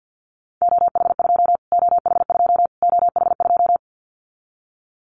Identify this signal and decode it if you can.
Morse code, heard as a keyed tone, about 50 wpm, 700 Hz. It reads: OH2 OH2 OH2
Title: Callsign Prefixes for Top 25 DXCC - Repeated 3x - 50wpm